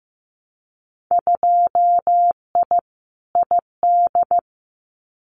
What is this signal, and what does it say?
Morse code, heard as a keyed tone, about 15 wpm, 700 Hz. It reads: 2I ID
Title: Winter Field Day Exchanges Encore 15wpm